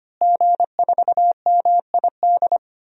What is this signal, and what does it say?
Morse code, heard as a keyed tone, about 25 wpm, 700 Hz. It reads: G4MID
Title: DXCC-100 Call Signs - 5 Characters - Repeated 3x - 25wpm